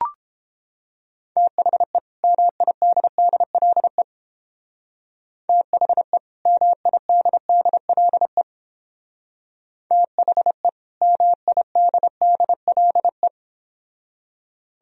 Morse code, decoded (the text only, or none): THE MIDDLE THE MIDDLE THE MIDDLE